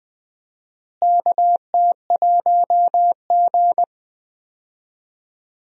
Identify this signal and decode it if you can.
Morse code, heard as a keyed tone, about 20 wpm, 700 Hz. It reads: KT1G
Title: USA Call Signs - 4 Characters - Repeated 2x - 20wpm